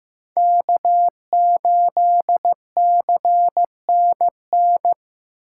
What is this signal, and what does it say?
Morse code, heard as a keyed tone, about 15 wpm, 700 Hz. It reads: K8CNN
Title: USA Call Signs - 6x Character Spacing - 15wpm